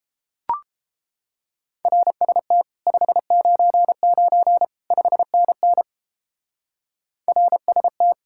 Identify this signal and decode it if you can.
Morse code, heard as a keyed tone, about 29 wpm, 700 Hz. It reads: RST 599 5NN RST